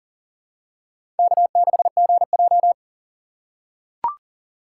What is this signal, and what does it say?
Morse code, heard as a keyed tone, about 40 wpm, 690 Hz. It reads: K6GJ
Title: USA Call Signs - 4 Characters - 2x Character Spacing - 40wpm